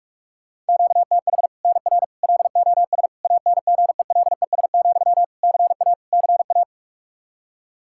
Morse code, heard as a keyed tone, about 47 wpm, 700 Hz. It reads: QTH NR LOS ANGELES, CA CA
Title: Practice QSO - Exchange 1 - Beginning - 45wpm